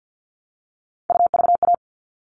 Morse code, heard as a keyed tone, about 50 wpm, 700 Hz. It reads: UVA
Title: All Cut Numbers - 3-Digit 50wpm